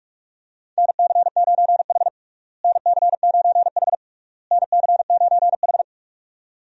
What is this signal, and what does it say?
Morse code, heard as a keyed tone, about 46 wpm, 700 Hz. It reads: NC9H NC9H NC9H
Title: USA Call Signs - 4 Characters - Repeated 3x - 45wpm